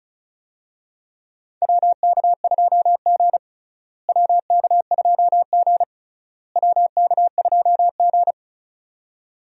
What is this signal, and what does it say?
Morse code, heard as a keyed tone, about 36 wpm, 700 Hz. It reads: WK2G WK2G WK2G
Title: USA Call Signs - 4 Characters - Repeated 3x - 35wpm